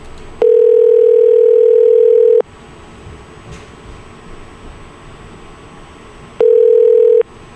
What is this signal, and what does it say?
Ringback tone (North America), 440 + 480 Hz.